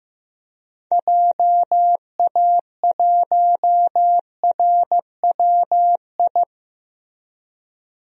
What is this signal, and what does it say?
Morse code, heard as a keyed tone, about 15 wpm, 700 Hz. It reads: JA1RWI
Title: DXCC-100 Call Signs - 6 Characters - 15wpm